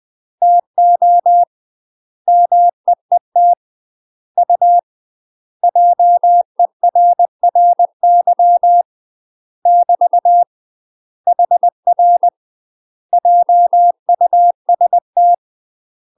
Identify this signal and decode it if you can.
Morse code, heard as a keyed tone, about 20 wpm, 700 Hz. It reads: TO MEET U JERRY <BT> HR JUST
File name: UK MASTER QSOs TRAINING / 20WPM / UK MASTER QSOs 0038